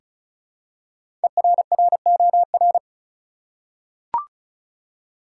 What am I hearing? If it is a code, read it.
Morse code, heard as a keyed tone, about 34 wpm, 700 Hz. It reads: ERROR
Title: Five Letter Words - 35wpm